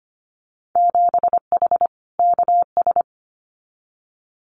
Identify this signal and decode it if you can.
Morse code, heard as a keyed tone, about 25 wpm, 700 Hz. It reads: 7H KS
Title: Winter Field Day Exchanges Encore 25wpm